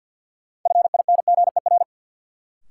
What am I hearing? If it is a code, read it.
Morse code, heard as a keyed tone, about 53 wpm, 700 Hz. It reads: FINGER
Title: Sentences from Top 1600 Words - New Word Drill Alt - 50wpm